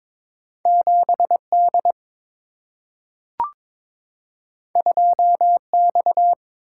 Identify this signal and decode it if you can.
Morse code, heard as a keyed tone, about 22 wpm, 700 Hz. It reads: 7D 2X
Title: Single-Letter Single-Number 22wpm